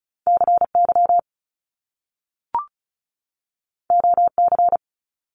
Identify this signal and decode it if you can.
Morse code, heard as a keyed tone, about 36 wpm, 700 Hz. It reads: CY OC
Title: Top Two Letter Combinations in Words 35wpm